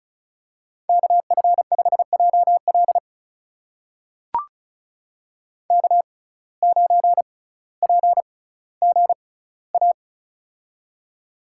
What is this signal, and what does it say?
Morse code, heard as a keyed tone, about 34 wpm, spaced as for 18 wpm, 700 Hz. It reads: KF5JL K9PGA
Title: USA Call Signs - 5 Characters - 6x Character Spacing - 35wpm